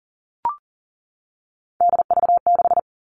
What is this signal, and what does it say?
Morse code, heard as a keyed tone, about 40 wpm, 700 Hz. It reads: DV6
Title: All Cut Numbers - 3-Digit 40wpm